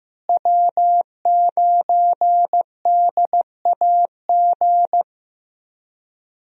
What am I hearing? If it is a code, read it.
Morse code, heard as a keyed tone, about 16 wpm, 700 Hz. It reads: W9DAG
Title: USA Call Signs - 5 Characters - Repeated 3x - 15wpm